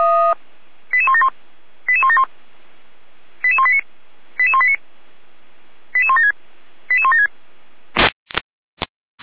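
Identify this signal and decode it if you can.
Five-tone selective call (ZVEI), heard as quick runs of five tones, tones of about 70 ms. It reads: 88171 88171 88189 88189 88176 88176
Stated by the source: ZVEI five-tone selective call, received by SDR.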